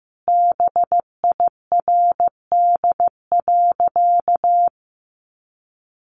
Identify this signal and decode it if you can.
Morse code, heard as a keyed tone, about 14 wpm, 700 Hz. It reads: BIRD.
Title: Sentences from Top 1100 Words 15wpm